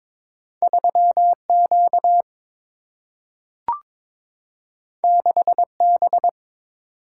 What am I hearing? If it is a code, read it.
Morse code, heard as a keyed tone, about 22 wpm, 700 Hz. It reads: 3Q 6B
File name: practice / 22wpm / Two Letter-Number Combinations - 22wpm